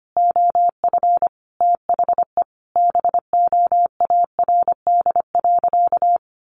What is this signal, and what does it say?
Morse code, heard as a keyed tone, about 25 wpm, 700 Hz. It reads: OF THE BOARD.
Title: Sentences from Top 1100 Words 25wpm